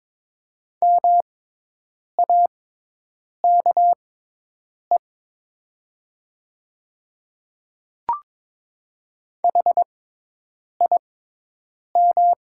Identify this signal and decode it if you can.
Morse code, heard as a keyed tone, about 22 wpm, spaced as for 8 wpm, 700 Hz. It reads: MAKE HIM
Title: 6x Character Spacing - Top 100 Words - 22wpm